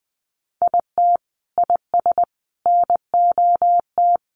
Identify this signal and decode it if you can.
Morse code, heard as a keyed tone, about 19 wpm, 700 Hz. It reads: IT IS NOT
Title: Sentences from Top 700 Words 20wpm